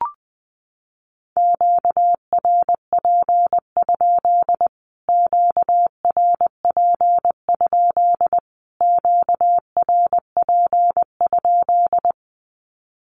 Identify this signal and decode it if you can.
Morse code, heard as a keyed tone, about 20 wpm, 700 Hz. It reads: QRP? QRP? QRP?
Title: Q-Codes Repeated 3x 20wpm